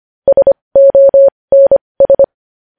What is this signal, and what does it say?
Morse code, heard as a keyed tone, about 25 wpm, spaced as for 20 wpm, 550 Hz. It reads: SONS
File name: podcast / News headlines-1x-25x20